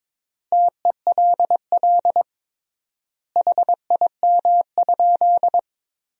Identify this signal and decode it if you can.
Morse code, heard as a keyed tone, about 22 wpm, 700 Hz. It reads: TELL HIM?
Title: 3x Word-Spacing - Sentences from Top 200 Words - 22wpm